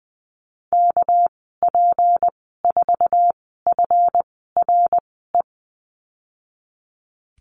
Morse code, heard as a keyed tone, about 20 wpm, spaced as for 15 wpm, 700 Hz. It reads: KP4FRE
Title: 200 Lesson 32g - Letter k - Call Signs 20wpm